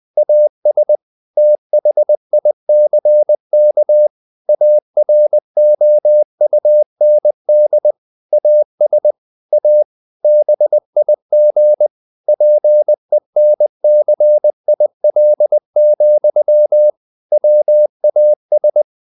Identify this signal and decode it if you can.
Morse code, heard as a keyed tone, about 20 wpm, 600 Hz. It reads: AS THICK AROUND AS A BIG PENCIL, WAS
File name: the old man and the sea / the old man and the sea part 1 20wpm